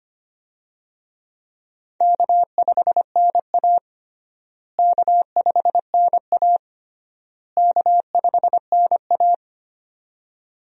Morse code, heard as a keyed tone, about 25 wpm, 700 Hz. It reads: K5NA K5NA K5NA
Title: USA Call Signs - 4 Characters - Repeated 3x - 25wpm